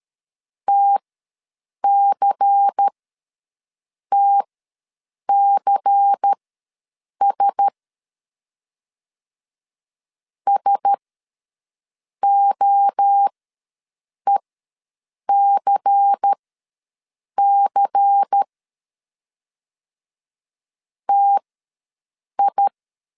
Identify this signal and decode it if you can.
Morse code, heard as a keyed tone, about 13 wpm, spaced as for 6 wpm, 800 Hz. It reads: TCTCS SOECC TI
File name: LEZIONI / 5 LBVC / 5 LBVC ws 12wpm 3159